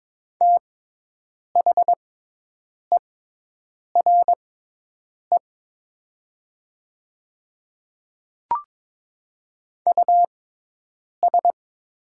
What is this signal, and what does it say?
Morse code, heard as a keyed tone, about 22 wpm, spaced as for 8 wpm, 700 Hz. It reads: THERE US
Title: 6x Character Spacing - Top 200 Words - 22wpm